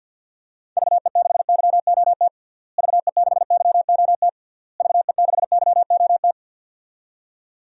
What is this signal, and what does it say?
Morse code, heard as a keyed tone, about 52 wpm, 700 Hz. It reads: VE6YOT VE6YOT VE6YOT
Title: DXCC-100 Call Signs - 6 Characters - Repeated 3x - 50wpm